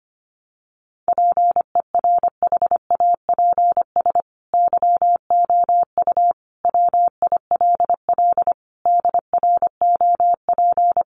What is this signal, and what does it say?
Morse code, heard as a keyed tone, about 25 wpm, 700 Hz. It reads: PERHAPS YOU WILL DROP
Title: Sentences from Top 1400 Words 25wpm